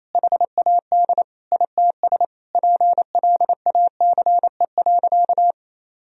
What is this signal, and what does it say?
Morse code, heard as a keyed tone, about 28 wpm, 700 Hz. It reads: HAD ITS PLACE.
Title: Sentences from Top 1000 Words - Review Part 1 - 28wpm